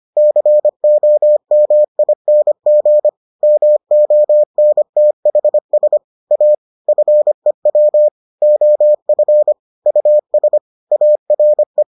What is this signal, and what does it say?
Morse code, heard as a keyed tone, about 25 wpm, 600 Hz. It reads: COMING MONTHS A FEW OF US ARE